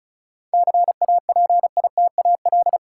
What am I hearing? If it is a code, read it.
Morse code, heard as a keyed tone, about 36 wpm, 700 Hz. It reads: CAPITAL
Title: Sentences from Top 1000 Words - New Word Drill - 35wpm